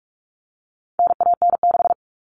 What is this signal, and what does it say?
Morse code, heard as a keyed tone, about 44 wpm, 700 Hz. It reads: NAN6
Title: All Cut Numbers - 4-Digit 45wpm